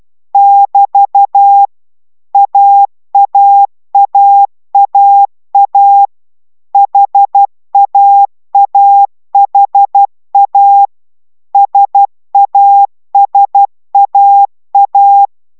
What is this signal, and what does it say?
Morse code, heard as a keyed tone, about 12 wpm, 800 Hz. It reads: <BT> AAAAA HAAHA SASAA
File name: taviro tanulas / cw 69 04 10